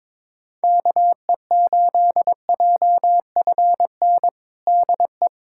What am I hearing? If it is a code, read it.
Morse code, heard as a keyed tone, about 22 wpm, 700 Hz. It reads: KE8JFN DE
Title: USA Callsign DE Callsign - 6 Characters - 22wpm